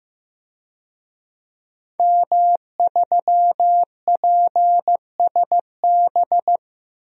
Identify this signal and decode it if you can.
Morse code, heard as a keyed tone, about 15 wpm, 700 Hz. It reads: M3PSB
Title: DXCC-100 Call Signs - 5 Characters - Repeated 3x - 15wpm